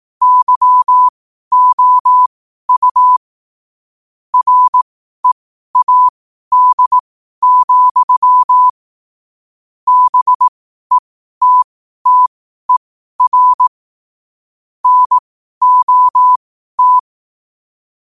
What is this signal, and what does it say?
Morse code, heard as a keyed tone, about 18 wpm, spaced as for 12 wpm, 1000 Hz. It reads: YOU READ, BETTER NOT
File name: qotdfiles / 12wpm